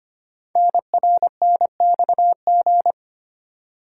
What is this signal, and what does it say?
Morse code, heard as a keyed tone, about 25 wpm, 700 Hz. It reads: NRNXG
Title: Code Groups - 5 Letters - 25wpm